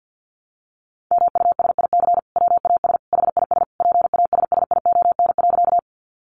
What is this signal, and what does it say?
Morse code, heard as a keyed tone, about 50 wpm, 700 Hz. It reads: MUSIC WAS HIS PASSION.
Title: Sentences from Top 1300 Words 50wpm